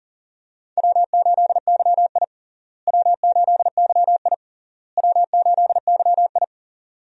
Morse code, heard as a keyed tone, about 40 wpm, 700 Hz. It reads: W8YI W8YI W8YI
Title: USA Call Signs - 4 Characters - Repeated 3x - 40wpm